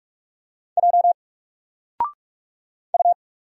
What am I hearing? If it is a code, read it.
Morse code, heard as a keyed tone, about 46 wpm, 700 Hz. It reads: J U
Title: Single Letter-Number 45wpm